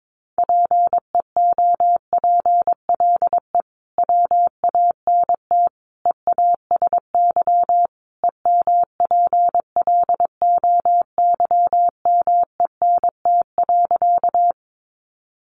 Morse code, heard as a keyed tone, about 22 wpm, 700 Hz. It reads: PEOPLE WANT EASY EMPLOYMENT.
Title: Sentences from Top 1500 Words 22wpm